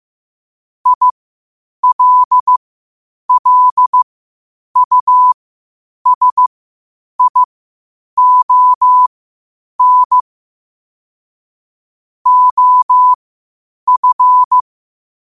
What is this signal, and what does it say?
Morse code, heard as a keyed tone, about 15 wpm, spaced as for 8 wpm, 1000 Hz. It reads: ILLUSION OF
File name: qotdfiles / 8wpm